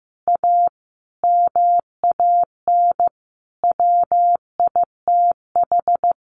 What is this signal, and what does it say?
Morse code, heard as a keyed tone, about 15 wpm, 700 Hz. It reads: A MAN WITH
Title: Sentences from Top 1400 Words 15wpm